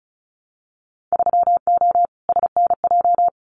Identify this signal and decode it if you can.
Morse code, heard as a keyed tone, about 34 wpm, 700 Hz. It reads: 3O SNJ
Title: Winter Field Day Exchanges Encore 35wpm